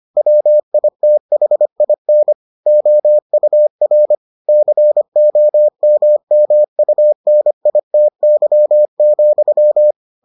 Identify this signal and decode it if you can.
Morse code, heard as a keyed tone, about 25 wpm, 600 Hz. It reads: WITHIN OUR COMMUNITY,